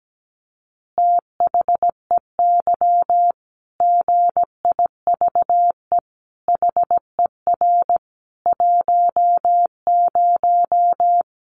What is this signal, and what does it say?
Morse code, heard as a keyed tone, about 17 wpm, 700 Hz. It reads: THEY GIVE HER 10
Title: Easy Phrases 17wpm